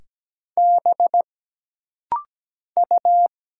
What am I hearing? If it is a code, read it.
Morse code, heard as a keyed tone, about 17 wpm, 700 Hz. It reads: B U
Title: Single Letter-Number Alt - Rapid-Fire 17wpm